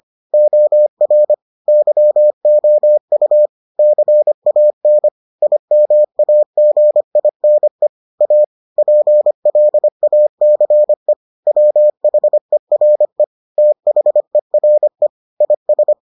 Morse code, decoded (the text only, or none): OR YOU CAN IMAGINE A PLACE WHERE THERE IS